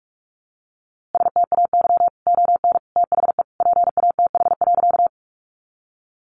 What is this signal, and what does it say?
Morse code, heard as a keyed tone, about 44 wpm, 700 Hz. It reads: STAY ON THE PATH.
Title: A Sound Of Thunder 45wpm